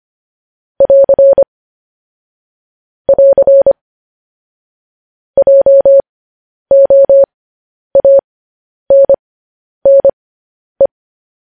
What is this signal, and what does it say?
Morse code, heard as a keyed tone, about 25 wpm, spaced as for 10 wpm, 550 Hz. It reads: <AR> <AR> JOANNE
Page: news headlines in morse code at 10 wpm